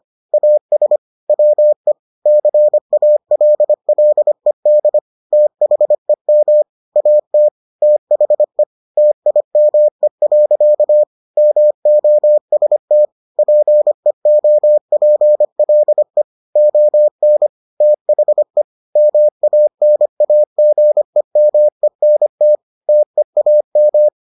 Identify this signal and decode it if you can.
Morse code, heard as a keyed tone, about 25 wpm, 600 Hz. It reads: AS WE CALLED THEM AT THE TIME. MOST PEOPLE ON THE MANAGEMENT TEAM